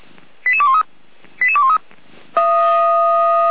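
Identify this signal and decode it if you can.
Five-tone selective call (ZVEI), heard as quick runs of five tones, tones of about 70 ms. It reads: 88213 88213
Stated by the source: ZVEI five-tone call received with an SDR.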